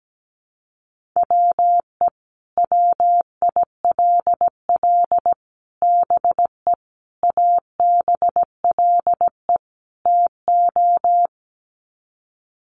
Morse code, heard as a keyed tone, about 17 wpm, 700 Hz. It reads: WE WILL BE ABLE TO
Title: Sets of 5 Words Encore 17wpm